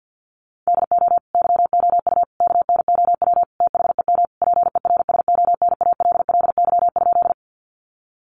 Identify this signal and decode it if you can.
Morse code, heard as a keyed tone, about 50 wpm, 700 Hz. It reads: DO YOU KNOW THEM PERSONALLY?
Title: Sentences from Top 1400 Words 50wpm